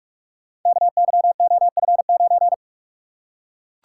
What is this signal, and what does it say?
Morse code, heard as a keyed tone, about 44 wpm, 690 Hz. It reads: KYOF9
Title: Code Groups - 5 Letter:Numbers - 45wpm